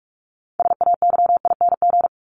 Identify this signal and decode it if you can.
Morse code, heard as a keyed tone, about 44 wpm, 700 Hz. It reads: SAYING
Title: ING Suffix - Top 100 Words with Base Words 45wpm